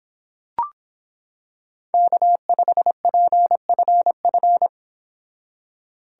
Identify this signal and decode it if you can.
Morse code, heard as a keyed tone, about 26 wpm, 700 Hz. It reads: K5PFF